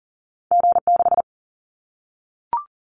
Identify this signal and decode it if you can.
Morse code, heard as a keyed tone, about 39 wpm, 700 Hz. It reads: G6
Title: Callsign Prefixes for Top 25 DXCC - Repeated 3x - 40wpm